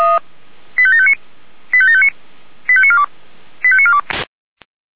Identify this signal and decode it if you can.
Five-tone selective call (ZVEI), heard as quick runs of five tones, tones of about 70 ms. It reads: 85759 85759 85942 85942